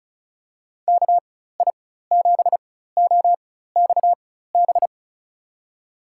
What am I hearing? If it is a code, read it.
Morse code, heard as a keyed tone, about 36 wpm, spaced as for 16 wpm, 700 Hz. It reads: KI7OXB
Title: USA Call Signs - 6 Characters - 4x Character Spacing - 35wpm